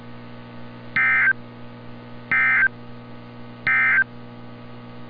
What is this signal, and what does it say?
Emergency Alert System (SAME) tones, heard as harsh buzzing bursts. It is heard from 0.8 to 4.1 s.